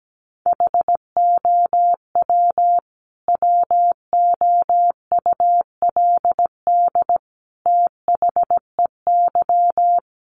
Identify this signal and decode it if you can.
Morse code, heard as a keyed tone, about 17 wpm, 700 Hz. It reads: HOW WOULD THEY
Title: Sentences from Top 300 Words 17wpm